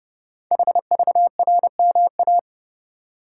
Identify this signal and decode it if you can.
Morse code, heard as a keyed tone, about 30 wpm, 690 Hz. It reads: HVRMA
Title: Code Groups - 5 Letters - 30wpm